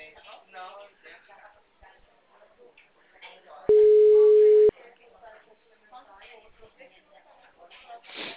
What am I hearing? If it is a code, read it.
Ringback tone (Europe, 425 Hz), with speech.
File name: out- -2003